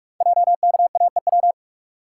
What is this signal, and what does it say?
Morse code, heard as a keyed tone, about 44 wpm, 700 Hz. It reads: JKAEW
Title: Code Groups - 5 Letters - 45wpm